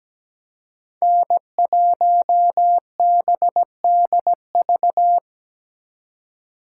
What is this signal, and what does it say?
Morse code, heard as a keyed tone, about 20 wpm, 700 Hz. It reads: N1BDV
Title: Speed Racing - USA 1x3 Call Signs 17wpm